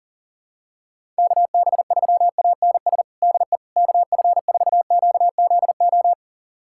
Morse code, heard as a keyed tone, about 40 wpm, 700 Hz. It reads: KB3ANS DE KF4QZO
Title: USA Callsign DE Callsign - 6 Characters - 40wpm